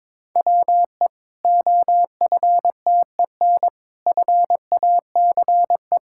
Morse code, heard as a keyed tone, about 22 wpm, 700 Hz. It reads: WE OFTEN FACE
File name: Difficult Sentences with Hints - 22wpm